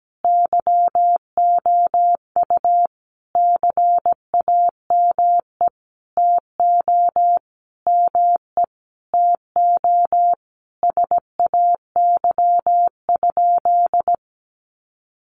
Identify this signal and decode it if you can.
Morse code, heard as a keyed tone, about 17 wpm, 700 Hz. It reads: YOU CAME TO ME TO SAY?
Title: Sentences from Top 200 Words 17wpm